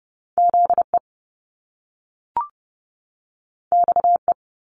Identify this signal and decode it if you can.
Morse code, heard as a keyed tone, about 30 wpm, 700 Hz. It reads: ZE XE
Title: Top Two Letter Combinations in Words 30wpm